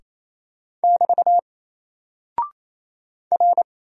Morse code, heard as a keyed tone, about 27 wpm, 700 Hz. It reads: <BT> R
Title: Single Character (Letter:Number:Punctuation:Prosign) Alt 28wpm